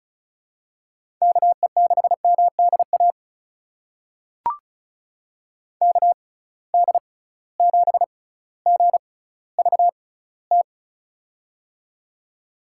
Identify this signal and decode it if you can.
Morse code, heard as a keyed tone, about 36 wpm, spaced as for 18 wpm, 700 Hz. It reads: KE6MDA KD7GVT
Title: USA Call Signs - 6 Characters - 6x Character Spacing - 35wpm